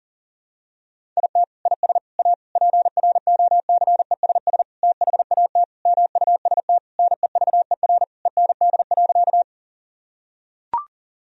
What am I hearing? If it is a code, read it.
Morse code, heard as a keyed tone, about 40 wpm, 700 Hz. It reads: IT IS A PROCESS THAT MUST NEVER END.
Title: Sentences from Top 1000 Words - Review Part 3 - 40wpm